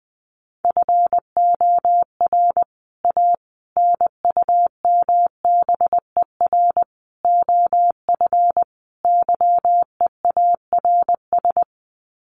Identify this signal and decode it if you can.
Morse code, heard as a keyed tone, about 20 wpm, 700 Hz. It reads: FOR A NUMBER OF YEARS
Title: Sets of 5 Words Encore 20wpm